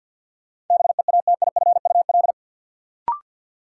Morse code, heard as a keyed tone, about 52 wpm, 700 Hz. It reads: BEATIFUL
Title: Sentences from Top 1300 Words - New Word Drill Alt - 50wpm